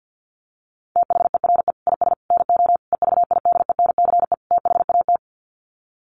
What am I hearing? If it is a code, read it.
Morse code, heard as a keyed tone, about 50 wpm, 700 Hz. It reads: THERE IS NO EVIDENCE THAT
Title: Sets of 5 Words Encore 50wpm